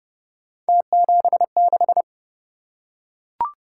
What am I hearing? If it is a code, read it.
Morse code, heard as a keyed tone, about 25 wpm, 700 Hz. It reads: T76
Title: Speed-Racing - Three Letter-Number Combinations - 20wpm